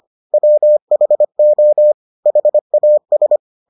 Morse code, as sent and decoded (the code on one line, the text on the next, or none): .-- .... --- / .... .- ...
WHO HAS